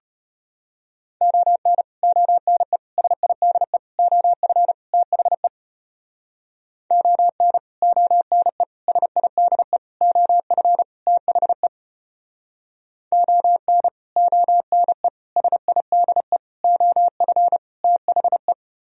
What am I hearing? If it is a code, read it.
Morse code, heard as a keyed tone, about 33 wpm, 700 Hz. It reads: ON ONE SIDE OF THE ON ONE SIDE OF THE ON ONE SIDE OF THE